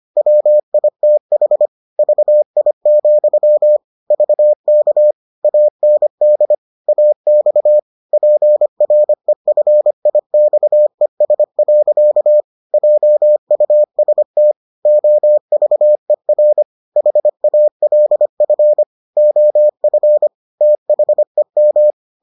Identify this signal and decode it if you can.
Morse code, heard as a keyed tone, about 25 wpm, 600 Hz. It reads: WITH VI, VK AND AX PREFIXES. JUST OVER HALF OF THEM